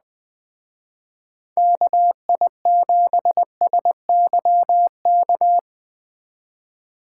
Morse code, heard as a keyed tone, about 21 wpm, 700 Hz. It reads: KI7SYK
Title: Speed Racing - USA 2x3 Call Signs 17wpm